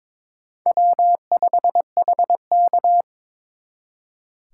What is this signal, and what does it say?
Morse code, heard as a keyed tone, about 26 wpm, 700 Hz. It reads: W5HK
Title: Speed Racing - USA 1x2 Call Signs 22wpm